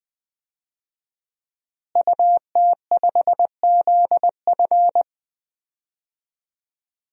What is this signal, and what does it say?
Morse code, heard as a keyed tone, about 20 wpm, 700 Hz. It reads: UT5ZF
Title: DXCC-100 Call Signs - 5 Characters - Repeated 3x - 20wpm